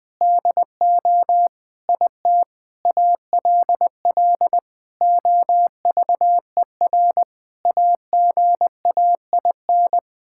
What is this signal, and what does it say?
Morse code, heard as a keyed tone, about 20 wpm, 700 Hz. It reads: DO IT ALL OVER AGAIN
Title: Sets of 9 Words - 20wpm